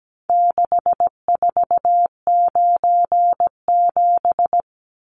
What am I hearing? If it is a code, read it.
Morse code, heard as a keyed tone, about 17 wpm, 700 Hz. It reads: 6497
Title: Numbers - 4-Digit 17wpm